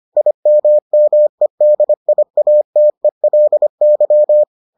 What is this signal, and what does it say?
Morse code, heard as a keyed tone, about 25 wpm, 600 Hz. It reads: IMMEDIATELY